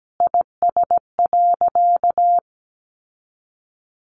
Morse code, heard as a keyed tone, about 17 wpm, 700 Hz. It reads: IS.
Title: Sentences from Top 700 Words 17wpm